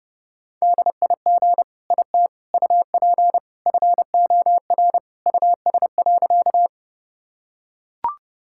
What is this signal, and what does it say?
Morse code, heard as a keyed tone, about 30 wpm, 700 Hz. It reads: DIG IT UP FOR US.
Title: Sentences from Top 1600 Words 30wpm